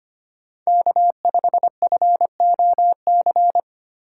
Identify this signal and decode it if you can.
Morse code, heard as a keyed tone, about 25 wpm, 700 Hz. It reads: K5FOC
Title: USA Call Signs - 5 Characters - 4x Character Spacing - 25wpm